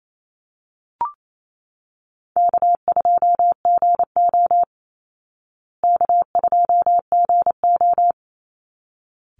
Morse code, heard as a keyed tone, about 28 wpm, 700 Hz. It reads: K2GO K2GO
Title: Easy Call Signs Repeated 2x - 28wpm